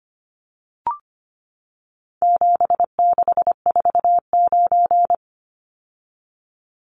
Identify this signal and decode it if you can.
Morse code, heard as a keyed tone, about 25 wpm, 700 Hz. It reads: 7649
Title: Numbers - 4-Digit 25wpm